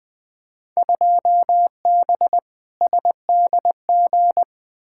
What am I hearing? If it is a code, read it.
Morse code, heard as a keyed tone, about 21 wpm, 700 Hz. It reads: 2B SDG
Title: Field Day Exchanges Encore - 20wpm